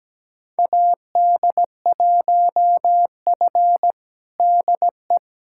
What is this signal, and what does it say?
Morse code, heard as a keyed tone, about 17 wpm, 700 Hz. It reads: AD1F DE
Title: USA Callsign DE Callsign - 4 Characters - 17wpm